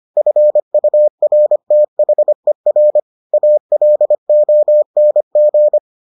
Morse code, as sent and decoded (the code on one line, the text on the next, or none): ..-. ..- .-. - .... .-. / .- .-.. --- -. --.
FURTHER ALONG